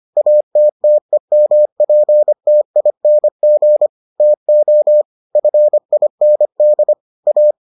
Morse code, heard as a keyed tone, about 25 wpm, 600 Hz. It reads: ATTEMPTING TO FIND A